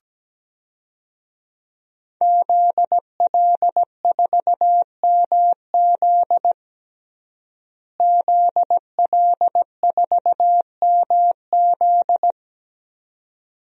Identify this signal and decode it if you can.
Morse code, heard as a keyed tone, about 17 wpm, 700 Hz. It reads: ZL4MZ ZL4MZ
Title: DXCC-100 Call Signs - 5 Characters - Repeated 2x - 17wpm